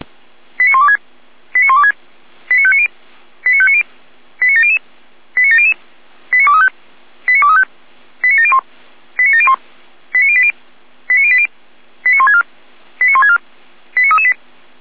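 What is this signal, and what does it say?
Five-tone selective call (ZVEI), heard as quick runs of five tones, tones of about 70 ms. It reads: 89126 89126 89590 89590 89700 89700 89235 89235 89701 89701 89080 89080 89164 89164 89338
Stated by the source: ZVEI five-tone call received by SDR.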